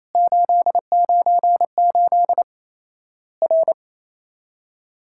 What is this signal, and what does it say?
Morse code, heard as a keyed tone, about 28 wpm, 700 Hz. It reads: 898 R
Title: Worked All Europe DX Contest - QTC Exchange - 28wpm